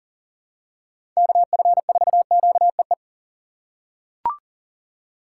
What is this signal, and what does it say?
Morse code, heard as a keyed tone, about 37 wpm, 700 Hz. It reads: KF4QEE
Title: USA Call Signs - 6 Characters - 2x Character Spacing - 40wpm